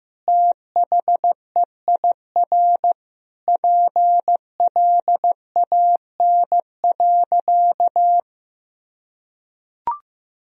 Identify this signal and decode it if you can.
Morse code, heard as a keyed tone, about 16 wpm, 700 Hz. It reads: THEIR PLAN.
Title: Sentences from Top 1000 Words - Review Part 2 - 15wpm